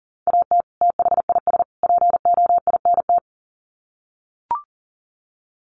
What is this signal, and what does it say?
Morse code, heard as a keyed tone, about 40 wpm, 700 Hz. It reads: AT THIS POINT
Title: Sets of 3 Words 40wpm